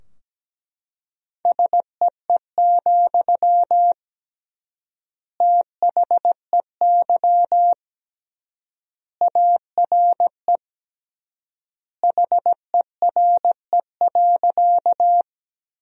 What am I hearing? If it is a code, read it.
Morse code, heard as a keyed tone, about 17 wpm, 700 Hz. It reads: SEE, THEY ARE HERE.
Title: 3x Word-Spacing - Sentences from Top 200 Words - 17wpm